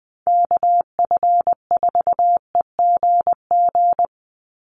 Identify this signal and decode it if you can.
Morse code, heard as a keyed tone, about 20 wpm, 700 Hz. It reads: KF4EGG
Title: USA Call Signs 20wpm